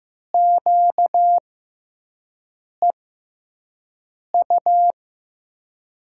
Morse code, heard as a keyed tone, about 16 wpm, spaced as for 5 wpm, 700 Hz. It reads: QEU
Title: USA Call Signs - 6 Characters - 6x Character Spacing - 15wpm